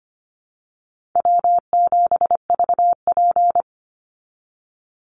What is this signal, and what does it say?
Morse code, heard as a keyed tone, about 25 wpm, 700 Hz. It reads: W7VP
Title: Call Signs Repeated 3x - 25wpm